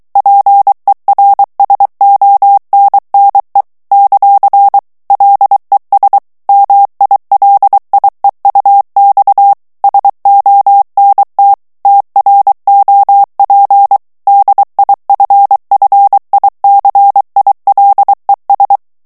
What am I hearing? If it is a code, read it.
Morse code, heard as a keyed tone, about 23 wpm, 800 Hz. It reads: PERSONNE ; LES MILIEUX SONT TROP DIFFICILES
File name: CW / FICHIERS-MP3 / FRANCAIS TEXTE CLAIR 2 VITESSE 22 MOTS MINUTES